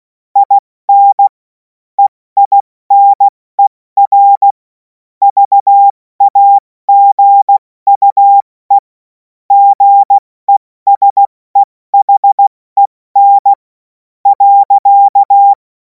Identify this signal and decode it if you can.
Morse code, heard as a keyed tone, about 16 wpm, 800 Hz. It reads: IN EINER VAGUE GESEHEN .
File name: Morsekurs / Witze / Witz 11. August 2026 70 F80